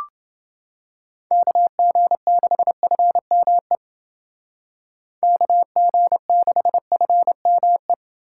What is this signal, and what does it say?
Morse code, heard as a keyed tone, about 30 wpm, 700 Hz. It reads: KG6FME KG6FME